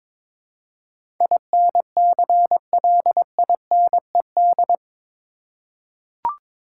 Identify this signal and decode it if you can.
Morse code, heard as a keyed tone, about 22 wpm, 700 Hz. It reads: INCLINED
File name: practice / 22wpm / Eight Letter Words - 22wpm